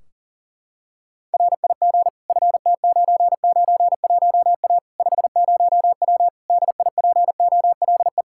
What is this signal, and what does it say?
Morse code, heard as a keyed tone, about 40 wpm, 700 Hz. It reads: RIG FT991A 50W DIPOLE
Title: Practice QSO - Exchange 2 - Info - 40wpm